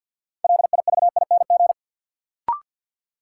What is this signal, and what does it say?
Morse code, heard as a keyed tone, about 53 wpm, 700 Hz. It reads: LIVING
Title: Sentences from Top 1000 Words - Review Part 6 - New Word Drill - 50wpm